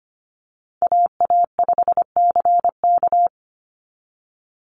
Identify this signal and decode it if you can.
Morse code, heard as a keyed tone, about 25 wpm, 700 Hz. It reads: AA5CK
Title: NA SOTA Chaser Callsigns 25wpm